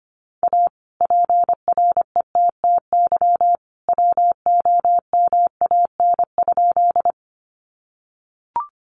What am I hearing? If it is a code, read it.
Morse code, heard as a keyed tone, about 25 wpm, 700 Hz. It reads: A PRETTY WOMAN?
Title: Sentences from Top 1100 Words 25wpm